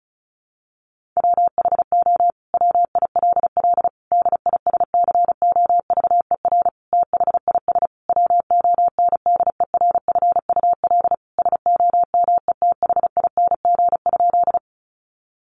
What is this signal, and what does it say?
Morse code, heard as a keyed tone, about 34 wpm, 700 Hz. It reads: WHO WILL DISCOVER THIS WONDERFUL SOMETHING?
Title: Sentences from Top 1100 Words 35wpm